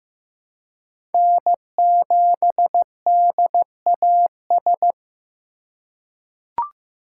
Morse code, heard as a keyed tone, about 15 wpm, 700 Hz. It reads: N7DAS
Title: USA Call Signs - 5 Characters - 2x Character Spacing - 15wpm